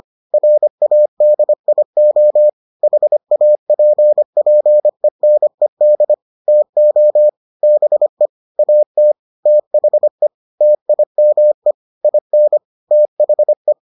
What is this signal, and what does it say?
Morse code, heard as a keyed tone, about 25 wpm, 600 Hz. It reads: RADIO HAPPENED TO BE AT THE TIME IN THE